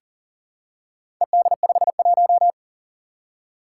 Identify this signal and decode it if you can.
Morse code, heard as a keyed tone, about 40 wpm, 700 Hz. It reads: ED51
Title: Four Challenging Letter-Number Combinations 40wpm